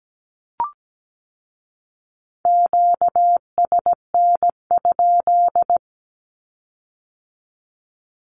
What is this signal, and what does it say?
Morse code, heard as a keyed tone, about 17 wpm, 700 Hz. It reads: QSN?
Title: Q-Codes with Spoken Meaning - 17wpm